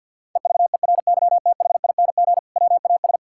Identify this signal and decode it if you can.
Morse code, heard as a keyed tone, about 50 wpm, 700 Hz. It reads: EVERYTHING WAS
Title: Sentences from Top 1600 Words 50wpm